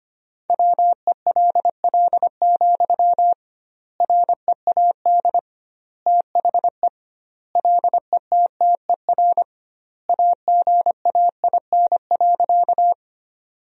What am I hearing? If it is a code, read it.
Morse code, heard as a keyed tone, about 25 wpm, 700 Hz. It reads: WELL, READ THE LETTER AGAIN.
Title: 2x Word-Spacing - Sentences from Top 300 Words - 25wpm